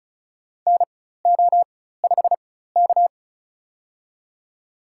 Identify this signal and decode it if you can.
Morse code, heard as a keyed tone, about 35 wpm, spaced as for 16 wpm, 700 Hz. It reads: NO5K
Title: USA Call Signs - 4 Characters - 4x Character Spacing - 35wpm